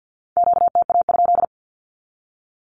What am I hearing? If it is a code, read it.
Morse code, heard as a keyed tone, about 50 wpm, 700 Hz. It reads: QTA?
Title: Q-Codes 50wpm